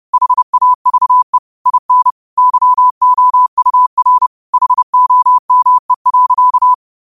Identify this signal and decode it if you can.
Morse code, heard as a keyed tone, about 30 wpm, 1000 Hz. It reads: HAVE IN YOUR HOME.
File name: qotdfiles / 30wpm